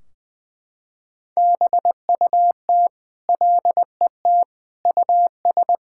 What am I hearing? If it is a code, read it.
Morse code, heard as a keyed tone, about 21 wpm, 700 Hz. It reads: BUT LET US
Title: Sentences from Top 1600 Words 20wpm